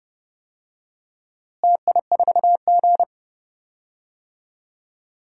Morse code, heard as a keyed tone, about 30 wpm, 700 Hz. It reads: TI4G
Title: DXCC-100 Call Signs - 4 Characters - Repeated 2x - 30wpm